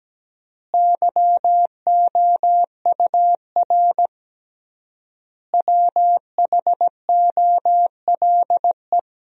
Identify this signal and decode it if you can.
Morse code, heard as a keyed tone, about 17 wpm, 700 Hz. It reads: YOUR WHOLE
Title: 3x Word-Spacing - Sentences from Top 400 Words - 17wpm